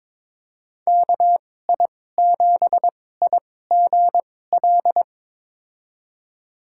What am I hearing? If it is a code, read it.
Morse code, heard as a keyed tone, about 22 wpm, spaced as for 16 wpm, 700 Hz. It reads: KI7IGL
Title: USA Call Signs - 6 Characters - 2x Character Spacing - 22wpm